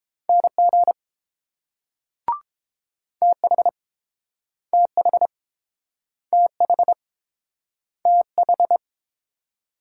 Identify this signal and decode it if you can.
Morse code, heard as a keyed tone, about 30 wpm, 690 Hz. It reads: NG TH TH TH TH